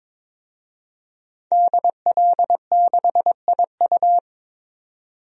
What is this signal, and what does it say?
Morse code, heard as a keyed tone, about 22 wpm, 700 Hz. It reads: DL6IU
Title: DXCC-100 Call Signs - 5 Characters - 22wpm